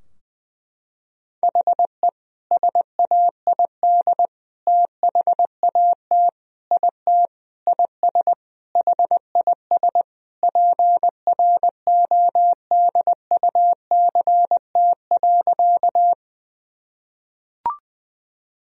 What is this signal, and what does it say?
Morse code, heard as a keyed tone, about 20 wpm, 700 Hz. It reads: HE SAID THAT IT IS HIS PRODUCT.
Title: Sentences from Top 1000 Words - Review Part 2 - 20wpm